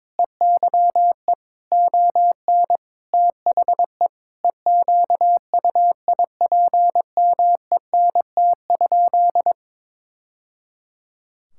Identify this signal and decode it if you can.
Morse code, heard as a keyed tone, about 22 wpm, 700 Hz. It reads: EYE ON THE EQUIPMENT?
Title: Sentences from Top 1000 Words - Review Part 5 - 22wpm